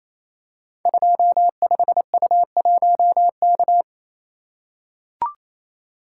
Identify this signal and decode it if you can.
Morse code, heard as a keyed tone, about 27 wpm, 700 Hz. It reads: 25U1K
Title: Code Groups - 5 Letter:Numbers - 28wpm